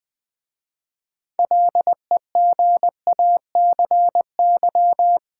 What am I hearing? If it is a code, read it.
Morse code, heard as a keyed tone, about 20 wpm, 700 Hz. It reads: LEGACY
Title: 6 Letter Words - First 3 Letters Given - 20wpm